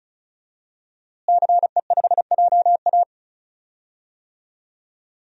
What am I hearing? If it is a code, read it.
Morse code, heard as a keyed tone, about 34 wpm, 700 Hz. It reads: CE5JA
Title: DXCC-100 Call Signs - 5 Characters - Repeated 3x - 35wpm